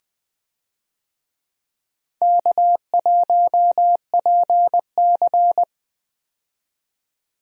Morse code, heard as a keyed tone, about 20 wpm, 700 Hz. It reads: K1PC
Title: USA Call Signs - 4 Characters - 20wpm